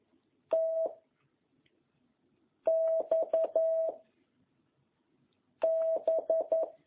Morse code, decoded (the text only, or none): T X B